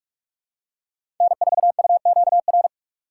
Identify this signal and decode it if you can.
Morse code, heard as a keyed tone, about 46 wpm, 700 Hz. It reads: N4UQR
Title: USA Call Signs - 5 Characters - 2x Character Spacing - 45wpm